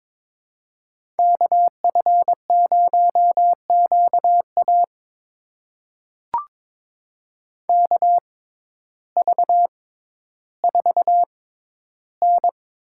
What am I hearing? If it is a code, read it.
Morse code, heard as a keyed tone, about 22 wpm, spaced as for 12 wpm, 700 Hz. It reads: KF0QA KV4N
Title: USA Call Signs - 5 Characters - 6x Character Spacing - 22wpm